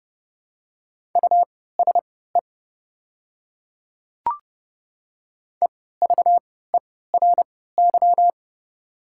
Morse code, heard as a keyed tone, about 30 wpm, spaced as for 17 wpm, 700 Hz. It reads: USE EVERY
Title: 3x Character Spacing - Top 300 Words - 30wpm